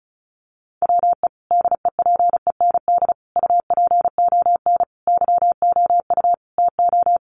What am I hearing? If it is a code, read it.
Morse code, heard as a keyed tone, about 34 wpm, 700 Hz. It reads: WE DEPEND UPON YOU TO
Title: Sentences from Top 1300 Words 35wpm